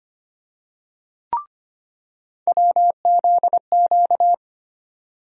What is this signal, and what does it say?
Morse code, heard as a keyed tone, about 25 wpm, 700 Hz. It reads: WZQ
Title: USA Call Sign Postfixes - 3 Letters 25wpm